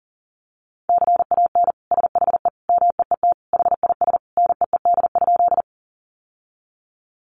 Morse code, heard as a keyed tone, about 40 wpm, 700 Hz. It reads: CAN SHE MEET HIS NEED?
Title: Sentences from Top 1100 Words 40wpm